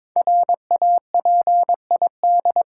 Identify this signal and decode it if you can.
Morse code, heard as a keyed tone, about 22 wpm, 690 Hz. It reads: RAPID
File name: 5 Letter Words - First 3 Letters Given - 22wpm